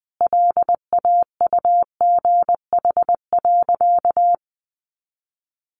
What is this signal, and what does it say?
Morse code, heard as a keyed tone, about 20 wpm, 700 Hz. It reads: LAUGH.
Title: Sentences from Top 600 Words 20wpm